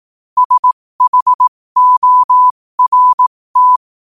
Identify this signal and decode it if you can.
Morse code, heard as a keyed tone, about 18 wpm, spaced as for 15 wpm, 1000 Hz. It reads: SHORT
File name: qotdfiles / 15wpm